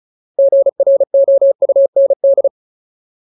Morse code, heard as a keyed tone, about 34 wpm, 550 Hz. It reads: GROUND
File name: Ultimate Word List / 35WPM Ultimate